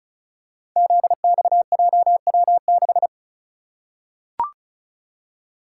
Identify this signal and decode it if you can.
Morse code, heard as a keyed tone, about 36 wpm, 700 Hz. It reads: ZXJW6
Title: Code Groups - 5 Letter:Numbers - 35wpm